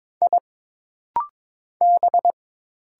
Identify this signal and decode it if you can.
Morse code, heard as a keyed tone, about 22 wpm, 700 Hz. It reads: I B
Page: This rapid-fire timing is geared to push your ICR (Instant Character Recognition) skills.